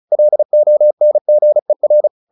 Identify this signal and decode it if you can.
Morse code, heard as a keyed tone, about 34 wpm, 600 Hz. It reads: LONGER
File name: the old man and the sea / the old man and the sea 1 35wpm